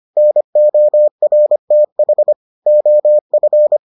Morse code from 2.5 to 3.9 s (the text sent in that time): OF